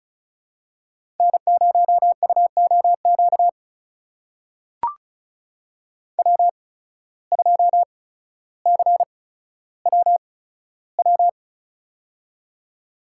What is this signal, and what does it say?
Morse code, heard as a keyed tone, about 36 wpm, spaced as for 15 wpm, 700 Hz. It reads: N0UOQ W2CWW
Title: USA Call Signs - 5 Characters - 8x Character Spacing - 35wpm